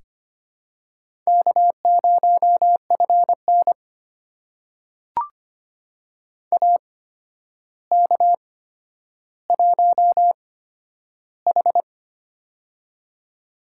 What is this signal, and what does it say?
Morse code, heard as a keyed tone, about 25 wpm, spaced as for 11 wpm, 700 Hz. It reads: K0FN AK1H
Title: USA Call Signs - 4 Characters - 8x Character Spacing - 25wpm